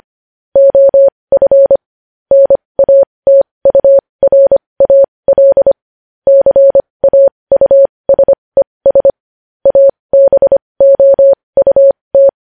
Morse code, heard as a keyed tone, about 25 wpm, spaced as for 20 wpm, 550 Hz. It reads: OF NATURAL CAUSES ABOUT
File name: podcast / News headlines-1x-25x20